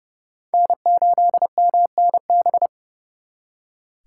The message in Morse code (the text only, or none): N8MNB